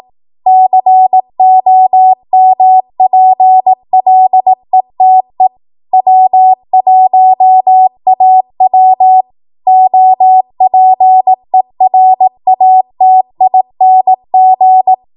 Morse code, heard as a keyed tone, about 18 wpm, 750 Hz. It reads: COMPLETE W1AW OPERATING
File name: Archive / 18 WPM / 18WPM